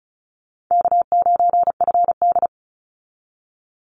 Morse code, heard as a keyed tone, about 34 wpm, 700 Hz. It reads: K9FD
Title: Easy Call Signs Repeated 3x - 35wpm